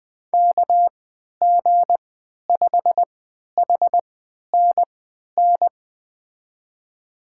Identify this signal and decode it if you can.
Morse code, heard as a keyed tone, about 20 wpm, spaced as for 11 wpm, 700 Hz. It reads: KG5HNN
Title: USA Call Signs - 3x Character Spacing - 20wpm